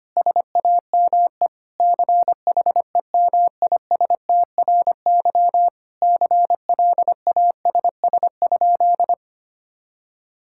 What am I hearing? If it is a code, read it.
Morse code, heard as a keyed tone, about 25 wpm, 700 Hz. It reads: SAME CHEMISTRY CLASS?
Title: Sentences from Top 1000 Words - Review Part 6 - 25wpm